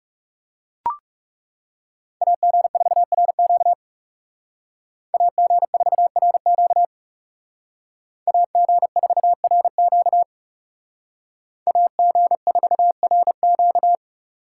Morse code, heard as a keyed tone, about 38 wpm, 700 Hz. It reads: AG4RQ AG4RQ AG4RQ AG4RQ